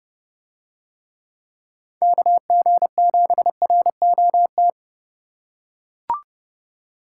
Morse code, heard as a keyed tone, about 30 wpm, 700 Hz. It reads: KG7ROT